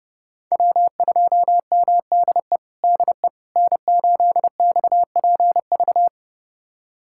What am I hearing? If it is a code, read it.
Morse code, heard as a keyed tone, about 30 wpm, 700 Hz. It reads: W2MDE DE N8XPV
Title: USA Callsign DE Callsign - 5 Characters - 30wpm